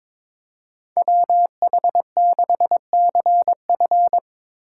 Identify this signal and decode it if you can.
Morse code, heard as a keyed tone, about 22 wpm, 700 Hz. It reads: WH6CF
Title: USA Call Signs - 3x Character Spacing - 22wpm